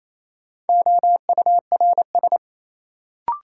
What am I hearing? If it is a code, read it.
Morse code, heard as a keyed tone, about 28 wpm, 700 Hz. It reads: OURS
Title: Sentences from Top 1300 Words - New Word Drill Alt - 28wpm